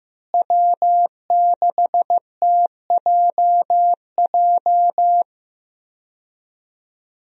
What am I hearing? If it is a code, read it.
Morse code, heard as a keyed tone, about 16 wpm, 700 Hz. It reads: W6TJJ
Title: USA Call Signs - 5 Characters - Repeated 3x - 15wpm